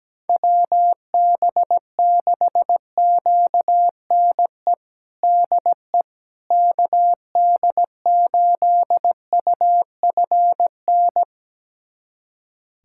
Morse code, heard as a keyed tone, about 17 wpm, 700 Hz. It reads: WB6QNE DE KD8UFN
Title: USA Callsign DE Callsign - 6 Characters - 17wpm